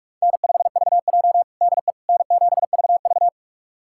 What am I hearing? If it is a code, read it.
Morse code, heard as a keyed tone, about 46 wpm, 700 Hz. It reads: N5VJ DE N7VV
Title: USA Callsign DE Callsign - 4 Characters - 45wpm